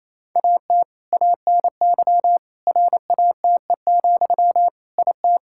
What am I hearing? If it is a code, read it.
Morse code, heard as a keyed tone, about 28 wpm, 700 Hz. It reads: AT ANY RATE, IT
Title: Sentences from Top 1600 Words 28wpm